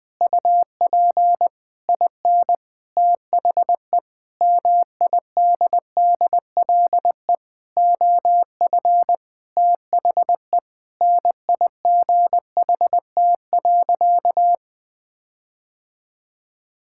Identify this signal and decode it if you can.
Morse code, heard as a keyed tone, about 21 wpm, 700 Hz. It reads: UP IN THE MIDDLE OF THE NIGHT.
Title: Sentences with 6-Word Phrases - Spoken Beginning - 20wpm